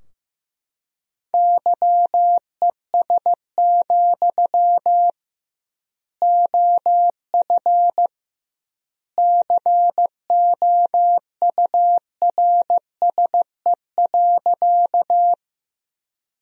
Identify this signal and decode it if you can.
Morse code, heard as a keyed tone, about 15 wpm, 700 Hz. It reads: YES, OF COURSE.
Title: 2x Word-Spacing - Sentences from Top 500 Words - 15wpm